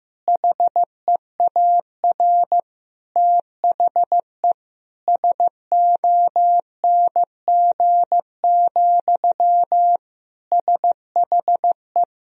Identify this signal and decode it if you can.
Morse code, heard as a keyed tone, about 15 wpm, 700 Hz. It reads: HEAR THE SONG, SHE
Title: Sentences from Top 1000 Words - Review Part 2 - 15wpm